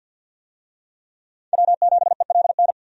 Morse code, heard as a keyed tone, about 51 wpm, 700 Hz. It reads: W7ELN
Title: USA Call Signs - 5 Characters - 50wpm